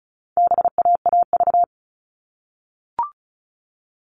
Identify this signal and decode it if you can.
Morse code, heard as a keyed tone, about 34 wpm, 700 Hz. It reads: BAAV
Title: All Cut Numbers - 4-Digit 35wpm